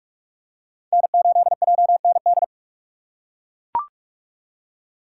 Morse code, heard as a keyed tone, about 42 wpm, 700 Hz. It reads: N8JND
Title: USA Call Signs - 4x Character Spacing - 45wpm